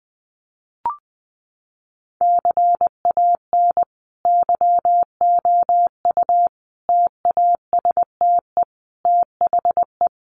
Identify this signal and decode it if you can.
Morse code, heard as a keyed tone, about 20 wpm, 700 Hz. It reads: CAN YOU TASTE THE
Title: Sentences from Top 1200 Words 20wpm